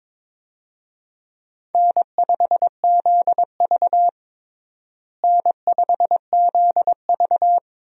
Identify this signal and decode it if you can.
Morse code, heard as a keyed tone, about 22 wpm, 700 Hz. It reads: N5ZV N5ZV
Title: USA Call Signs - 4 Characters - Repeated 3x - 22wpm